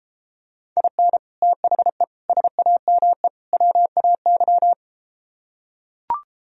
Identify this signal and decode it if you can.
Morse code, heard as a keyed tone, about 32 wpm, 700 Hz. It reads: IN THE SAME WAY